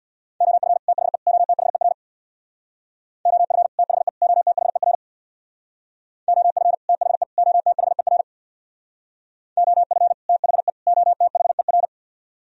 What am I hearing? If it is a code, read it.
Morse code, heard as a keyed tone, about 64 wpm, 700 Hz. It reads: OF THE OTHER OF THE OTHER OF THE OTHER OF THE OTHER